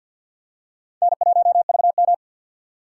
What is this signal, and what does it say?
Morse code, heard as a keyed tone, about 53 wpm, 700 Hz. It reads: N1VM
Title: USA Call Signs - 4 Characters - 6x Character Spacing - 50wpm